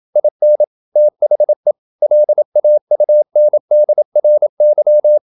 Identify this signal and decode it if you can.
Morse code, heard as a keyed tone, about 27 wpm, 600 Hz. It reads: IN THE LAUNDRY
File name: Never 0027wpm